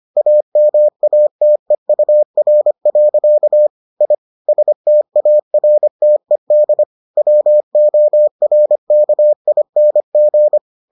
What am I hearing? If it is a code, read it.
Morse code, heard as a keyed tone, about 25 wpm, 600 Hz. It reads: AMATEUR. I STARTED WORKING